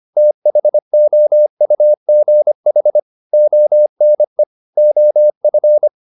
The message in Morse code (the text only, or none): THOUGH ONE OF